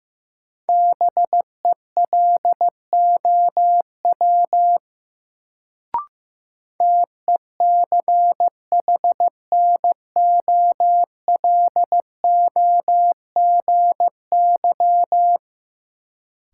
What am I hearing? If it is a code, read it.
Morse code, heard as a keyed tone, about 15 wpm, 700 Hz. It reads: BELOW TECHNOLOGY
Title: Sentences from Top 1000 Words - Review Part 5 - New Word Drill - 15wpm